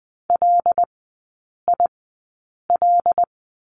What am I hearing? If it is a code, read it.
Morse code, heard as a keyed tone, about 20 wpm, spaced as for 8 wpm, 700 Hz. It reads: LIL
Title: 074 Lesson 13e - Letter L - Three Characters 20wpm